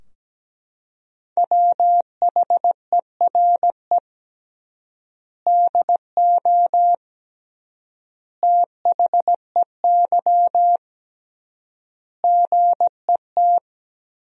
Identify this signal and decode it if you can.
Morse code, heard as a keyed tone, about 17 wpm, 700 Hz. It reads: WHERE DO THEY GET
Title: 3x Word-Spacing - Sentences from Top 200 Words - 17wpm